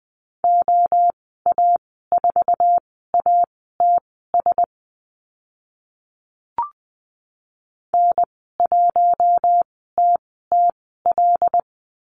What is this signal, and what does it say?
Morse code, heard as a keyed tone, about 20 wpm, spaced as for 14 wpm, 700 Hz. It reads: OA4ATS N1TTL
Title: 082 Lesson 14g - Number 2 - Call Signs 20wpm